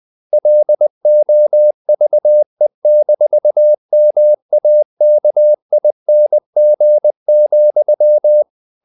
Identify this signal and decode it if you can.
Morse code, heard as a keyed tone, about 20 wpm, 600 Hz. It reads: LOVE-MAKING,
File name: the old man and the sea / the old man and the sea part 1 20wpm